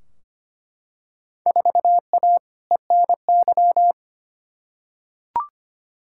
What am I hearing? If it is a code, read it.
Morse code, heard as a keyed tone, about 25 wpm, 700 Hz. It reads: 4A ENY
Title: Field Day Exchanges Encore - 25wpm